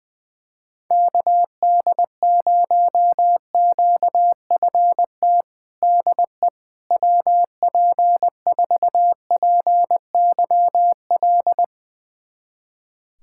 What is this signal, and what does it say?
Morse code, heard as a keyed tone, about 21 wpm, 700 Hz. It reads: KD0QFT DE WP4PYL
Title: USA Callsign DE Callsign - 6 Characters - 20wpm